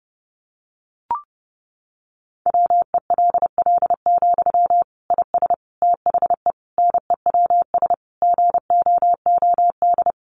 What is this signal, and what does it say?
Morse code, heard as a keyed tone, about 30 wpm, 700 Hz. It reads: WELL, IS THE NEWS GOOD
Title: Sentences from Top 800 Words 30wpm